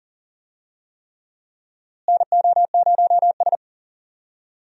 Morse code, heard as a keyed tone, about 40 wpm, 700 Hz. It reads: NO0S
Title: USA Call Signs - 4 Characters - Repeated 3x - 40wpm